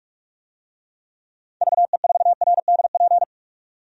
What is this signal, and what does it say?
Morse code, heard as a keyed tone, about 46 wpm, 700 Hz. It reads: VE4RDP
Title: DXCC-100 Call Signs - 6 Characters - 45wpm